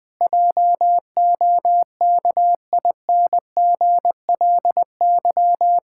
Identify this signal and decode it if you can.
Morse code, heard as a keyed tone, about 20 wpm, 700 Hz. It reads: JOKINGLY
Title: Pangram Challenge v2 - 20wpm